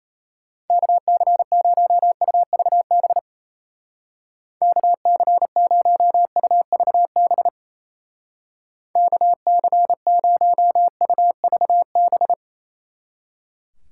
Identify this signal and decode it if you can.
Morse code, heard as a keyed tone, about 34 wpm, 700 Hz. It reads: KC0UVB KC0UVB KC0UVB
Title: Speed Racing - USA 2x3 Call Signs 28wpm